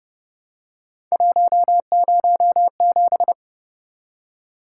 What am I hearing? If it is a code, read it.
Morse code, heard as a keyed tone, about 27 wpm, 700 Hz. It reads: 107